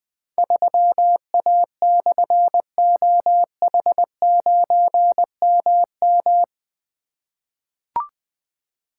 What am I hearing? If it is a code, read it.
Morse code, heard as a keyed tone, about 20 wpm, 700 Hz. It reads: 3A/OH9MM
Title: Difficult Call Signs 20wpm